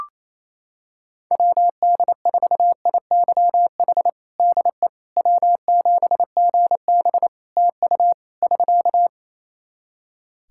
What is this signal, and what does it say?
Morse code, heard as a keyed tone, about 28 wpm, 700 Hz. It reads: WD4IYH DE W7GB TU <SK>
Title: Practice QSO - Exchange 4 - Ending - 28wpm